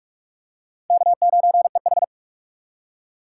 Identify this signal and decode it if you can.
Morse code, heard as a keyed tone, about 39 wpm, 700 Hz. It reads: K9EH